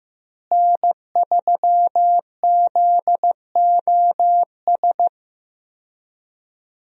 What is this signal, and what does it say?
Morse code, heard as a keyed tone, about 15 wpm, 700 Hz. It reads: N3ZOS
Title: USA Call Signs - 5 Characters - Repeated 2x - 15wpm